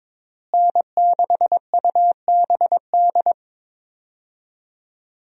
Morse code, heard as a keyed tone, about 22 wpm, 690 Hz. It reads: N6UBD
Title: USA Call Signs - 5 Characters - Repeated 3x - 22wpm